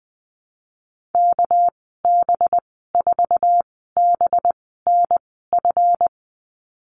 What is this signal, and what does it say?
Morse code, heard as a keyed tone, about 19 wpm, spaced as for 14 wpm, 700 Hz. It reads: KB4BNF
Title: 219 Lesson 35g - Number 0 - Call Signs 20wpm